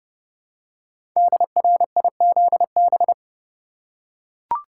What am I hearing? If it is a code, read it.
Morse code, heard as a keyed tone, about 30 wpm, 700 Hz. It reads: DRIZB
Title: Code Groups - 5 Letters - 30wpm